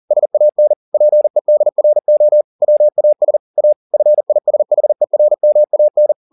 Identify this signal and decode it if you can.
Morse code, heard as a keyed tone, about 40 wpm, 600 Hz. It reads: SAN PEDRO WAS A FISHERMAN
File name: the old man and the sea 2 40wpm